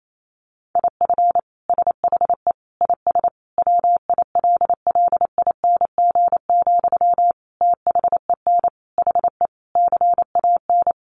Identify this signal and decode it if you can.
Morse code, heard as a keyed tone, about 28 wpm, 700 Hz. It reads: IF SHE IS WILLING, THEN HE CAN
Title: Sentences from Top 1000 Words 28wpm